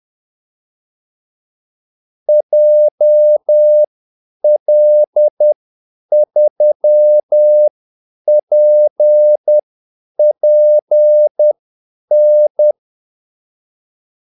Morse code, heard as a keyed tone, about 10 wpm, spaced as for 8 wpm, 600 Hz. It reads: JL3PPN
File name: Callsigns-10-8wpm